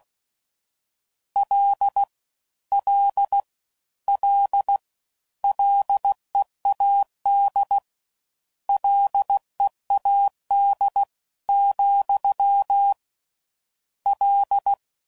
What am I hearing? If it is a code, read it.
Morse code, heard as a keyed tone, about 16 wpm, 800 Hz. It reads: L L L LEAD LEAD , L